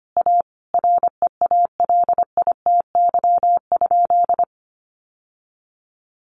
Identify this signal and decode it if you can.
Morse code, heard as a keyed tone, about 25 wpm, 700 Hz. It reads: A REALITY?
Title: Sentences from Top 1200 Words 25wpm